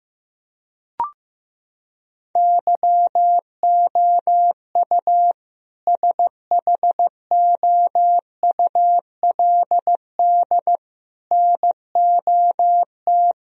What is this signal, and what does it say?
Morse code, heard as a keyed tone, about 15 wpm, 700 Hz. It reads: YOU SHOULD NOT
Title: Sentences from Top 1000 Words - Review Part 3 - 15wpm